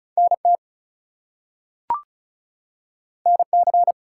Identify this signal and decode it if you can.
Morse code, heard as a keyed tone, about 36 wpm, 700 Hz. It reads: NT NC
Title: US States and Canadian Provinces 35wpm